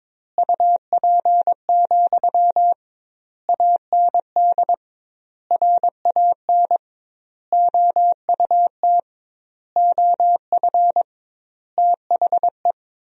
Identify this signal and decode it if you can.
Morse code, heard as a keyed tone, about 21 wpm, 700 Hz. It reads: UP, AND RAN OUT OF THE
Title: 2x Word-Spacing - Sentences from Top 500 Words - 22wpm